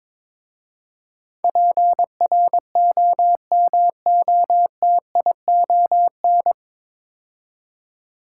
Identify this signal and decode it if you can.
Morse code, heard as a keyed tone, about 23 wpm, 700 Hz. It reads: PROMOTION
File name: 9 Letter Words - First 3 Letters Given - 22wpm